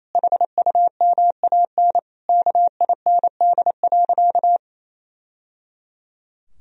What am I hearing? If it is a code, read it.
Morse code, heard as a keyed tone, about 28 wpm, 700 Hz. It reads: HUMAN KIND.
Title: Sentences from Top 1000 Words - Review Part 4 - 28wpm